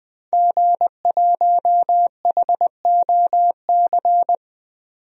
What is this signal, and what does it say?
Morse code, heard as a keyed tone, about 21 wpm, 700 Hz. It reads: G1HOC
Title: DXCC-100 Call Signs - 5 Characters - 20wpm